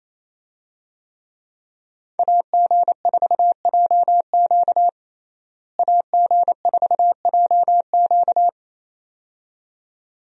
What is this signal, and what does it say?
Morse code, heard as a keyed tone, about 28 wpm, 700 Hz. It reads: AG4JQ AG4JQ
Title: USA Call Signs - 5 Characters - Repeated 2x - 28wpm